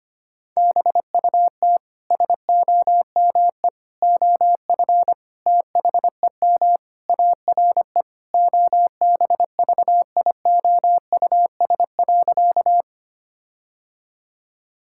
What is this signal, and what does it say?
Morse code, heard as a keyed tone, about 25 wpm, 700 Hz. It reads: BUT SOME OF THEM ARE OBVIOUS.
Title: Sentences from Top 1600 Words 25wpm